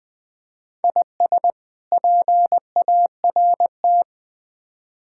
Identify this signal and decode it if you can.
Morse code, heard as a keyed tone, about 21 wpm, 700 Hz. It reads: IS PART
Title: Build A Phrase - 4-Word Phrases - 20wpm